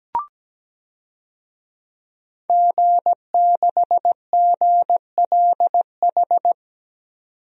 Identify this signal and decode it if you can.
Morse code, heard as a keyed tone, about 17 wpm, 700 Hz. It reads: G6GLH
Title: DXCC-100 Call Signs - 5 Characters - 17wpm